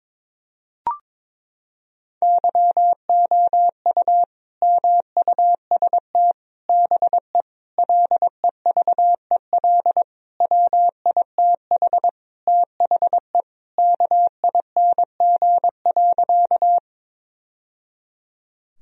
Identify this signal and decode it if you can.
Morse code, heard as a keyed tone, about 21 wpm, 700 Hz. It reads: YOU MUST BE LEVEL WITH THE KING.
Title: Sentences from Top 1000 Words - Review Part 4 - 22wpm